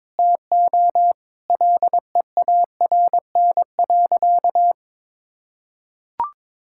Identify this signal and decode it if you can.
Morse code, heard as a keyed tone, about 22 wpm, 700 Hz. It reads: TO LEARN.
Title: Sentences from Top 1000 Words - Review Part 6 - 22wpm